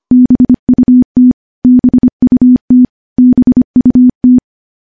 Morse code, heard as a keyed tone, about 25 wpm, 250 Hz. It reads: BUT BUT BUT
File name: morse / morse-training-25WPM